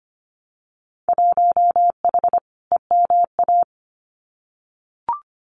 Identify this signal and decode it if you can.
Morse code, heard as a keyed tone, about 25 wpm, 700 Hz. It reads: 1H EMA
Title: Winter Field Day Exchanges Encore 25wpm